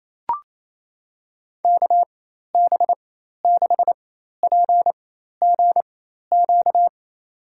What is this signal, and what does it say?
Morse code, heard as a keyed tone, about 27 wpm, spaced as for 13 wpm, 700 Hz. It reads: KB6PGQ
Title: USA Call Signs - 6 Characters - 4x Character Spacing - 28wpm